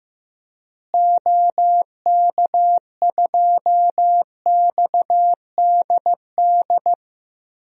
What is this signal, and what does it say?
Morse code, heard as a keyed tone, about 15 wpm, 700 Hz. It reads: OK2XDD
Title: DXCC-100 Call Signs - 6 Characters - Repeated 2x - 15wpm